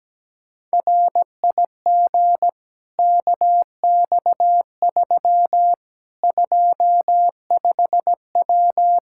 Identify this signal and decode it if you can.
Morse code, heard as a keyed tone, about 17 wpm, 700 Hz. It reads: RIG KX3 25W
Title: Practice QSO - Exchange 2 - Info - 17wpm